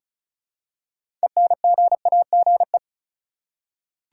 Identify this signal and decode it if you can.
Morse code, heard as a keyed tone, about 36 wpm, 700 Hz. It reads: ENGAGE
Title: Six Letter Words - 35wpm